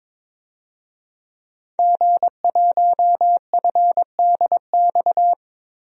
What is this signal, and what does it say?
Morse code, heard as a keyed tone, about 22 wpm, 700 Hz. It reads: G1FDX
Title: DXCC-100 Call Signs - 5 Characters - Repeated 3x - 22wpm